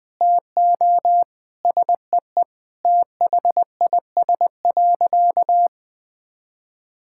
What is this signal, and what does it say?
Morse code, heard as a keyed tone, about 20 wpm, 700 Hz. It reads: TO SEE THIS.
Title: Sentences with 2-Word Phrases - 20wpm